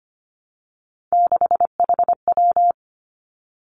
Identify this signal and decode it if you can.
Morse code, heard as a keyed tone, about 25 wpm, 700 Hz. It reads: 6HW
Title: Three Challenging Letter-Number Combinations 25wpm